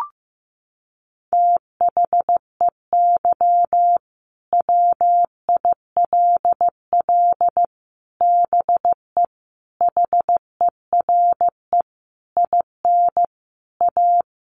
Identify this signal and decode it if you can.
Morse code, heard as a keyed tone, about 14 wpm, 700 Hz. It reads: THEY WILL BE HERE IN A
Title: Sentences from Top 1000 Words 15wpm